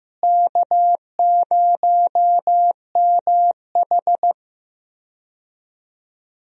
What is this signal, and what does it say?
Morse code, heard as a keyed tone, about 15 wpm, 700 Hz. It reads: K0MH
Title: USA Call Signs - 4 Characters - Repeated 3x - 15wpm